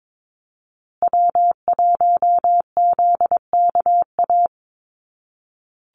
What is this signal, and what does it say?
Morse code, heard as a keyed tone, about 22 wpm, 700 Hz. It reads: W1ZKA
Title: USA Call Signs Repeated 3x 22wpm